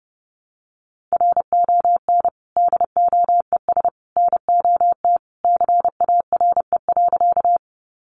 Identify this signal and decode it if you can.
Morse code, heard as a keyed tone, about 30 wpm, 700 Hz. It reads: RON DOES NOT CARE.
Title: Additional Easy Phrases 30wpm